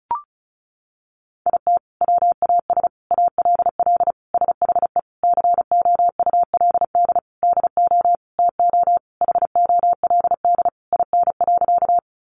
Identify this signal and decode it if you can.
Morse code, heard as a keyed tone, about 34 wpm, 700 Hz. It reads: IT WAS ALL SHE COULD DO TO HOLD IN.
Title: Sentences from Top 400 Words 35wpm